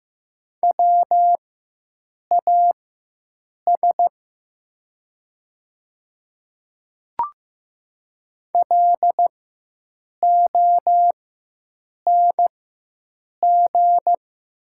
Morse code, heard as a keyed tone, about 15 wpm, spaced as for 7 wpm, 700 Hz. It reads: WAS LONG
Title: 4x Character Spacing - Top 200 Words - 15wpm